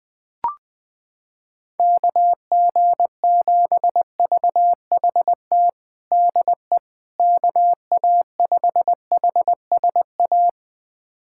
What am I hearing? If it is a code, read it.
Morse code, heard as a keyed tone, about 21 wpm, 700 Hz. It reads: KG7VHT DE KA5HSA
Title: USA Callsign DE Callsign - 6 Characters - 20wpm